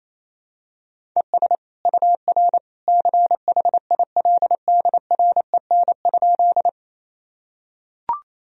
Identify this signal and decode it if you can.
Morse code, heard as a keyed tone, about 28 wpm, 700 Hz. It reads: ES UR CHILDREN?
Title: Sentences from Top 300 Words and Abbreviations 28wpm